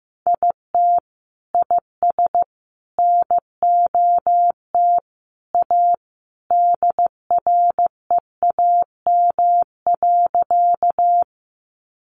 Morse code, heard as a keyed tone, about 15 wpm, 700 Hz. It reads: IT IS NOT A DREAM.
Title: Sentences from Top 1500 Words 15wpm